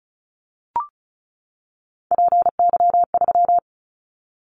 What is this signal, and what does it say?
Morse code, heard as a keyed tone, about 36 wpm, 700 Hz. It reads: PY3
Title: Callsign Prefixes for Top 100 DXCC - 35wpm